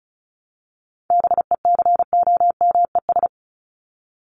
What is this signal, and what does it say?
Morse code, heard as a keyed tone, about 34 wpm, 700 Hz. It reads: BECOMES
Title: ES:S Suffix - Top 100 Words with Base Words 35wpm